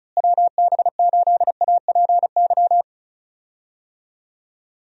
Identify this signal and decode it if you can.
Morse code, heard as a keyed tone, about 34 wpm, 700 Hz. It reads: WB8APY
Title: USA Call Signs - 6 Characters - Repeated 2x - 35wpm